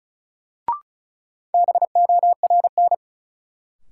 Morse code, heard as a keyed tone, about 34 wpm, 700 Hz. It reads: BORN
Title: Sentences from Top 1000 Words - New Word Drill - 35wpm